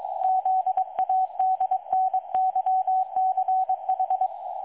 Morse code, heard as a keyed tone, about 22 wpm, 750 Hz. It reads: ZADNYCH
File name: dlazaawansowanych / nagraniazpasma / websdr recording 3565.0kHz